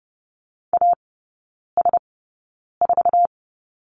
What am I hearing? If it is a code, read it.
Morse code, heard as a keyed tone, about 30 wpm, spaced as for 9 wpm, 700 Hz. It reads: AS4
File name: Lesson 18e - Letter m - Three Characters 30wpm